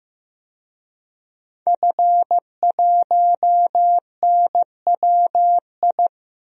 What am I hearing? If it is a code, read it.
Morse code, heard as a keyed tone, about 15 wpm, 700 Hz. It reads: F1NWI
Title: DXCC-100 Call Signs - 5 Characters - 15wpm